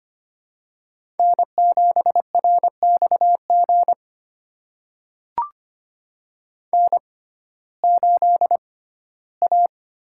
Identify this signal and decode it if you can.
Morse code, heard as a keyed tone, about 25 wpm, spaced as for 15 wpm, 700 Hz. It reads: N7RXG N8A
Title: USA Call Signs - 5 Characters - 6x Character Spacing - 25wpm